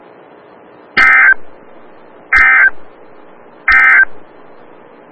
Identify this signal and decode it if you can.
Emergency Alert System (SAME) tones, heard as harsh buzzing bursts. It is heard from 0.9 to 4.1 s.
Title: Des Moines, IA NOAA Weather Radio - WXL57